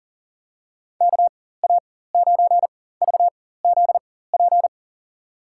Morse code, heard as a keyed tone, about 39 wpm, spaced as for 18 wpm, 700 Hz. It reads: KA9VZP
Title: USA Call Signs - 6 Characters - 4x Character Spacing - 40wpm